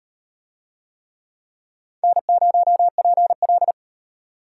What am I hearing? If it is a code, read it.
Morse code, heard as a keyed tone, about 34 wpm, 700 Hz. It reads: N0PL